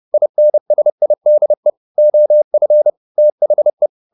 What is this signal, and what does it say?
Morse code, heard as a keyed tone, about 30 wpm, 600 Hz. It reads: INSIDE OF THE